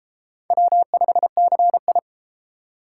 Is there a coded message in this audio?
Yes: Morse code from 0.5 to 2.0 s, heard as a keyed tone, about 32 wpm, 700 Hz. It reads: W5CI